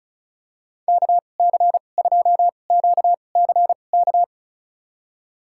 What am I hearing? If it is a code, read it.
Morse code, heard as a keyed tone, about 36 wpm, spaced as for 25 wpm, 700 Hz. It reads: KC2QCK
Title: USA Call Signs - 2x Character Spacing - 35wpm